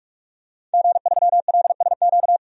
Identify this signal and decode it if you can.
Morse code, heard as a keyed tone, about 48 wpm, 700 Hz. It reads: G3LSQ
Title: DXCC-100 Call Signs - 5 Characters - 45wpm